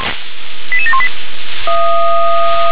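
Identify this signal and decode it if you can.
Five-tone selective call (ZVEI), heard as quick runs of five tones, tones of about 70 ms. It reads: 88718
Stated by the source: received on an SDR; ZVEI five-tone call 88771, 357 directory.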